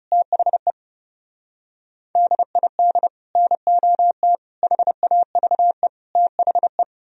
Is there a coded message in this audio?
Yes: Morse code from 0.1 to 6.8 s, heard as a keyed tone, about 38 wpm, 700 Hz. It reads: THE DID NOT HAVE THE